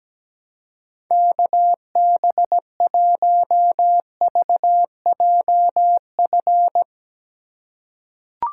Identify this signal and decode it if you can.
Morse code, heard as a keyed tone, about 17 wpm, 700 Hz. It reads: KB1VJF
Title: USA Call Signs - 6 Characters - 2x Character Spacing - 17wpm